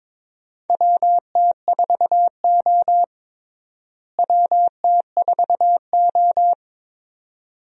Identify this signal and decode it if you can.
Morse code, heard as a keyed tone, about 22 wpm, 700 Hz. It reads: WT4O WT4O
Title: USA Call Signs - 4 Characters - Repeated 2x - 22wpm